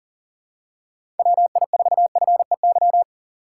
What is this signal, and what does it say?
Morse code, heard as a keyed tone, about 39 wpm, 700 Hz. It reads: WI4FEY
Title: USA Call Signs - 6 Characters - 3x Character Spacing - 40wpm